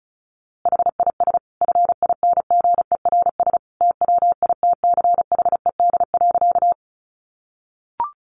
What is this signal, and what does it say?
Morse code, heard as a keyed tone, about 34 wpm, 700 Hz. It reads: HIS FINGERS TWITCHED.
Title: A Sound Of Thunder 35wpm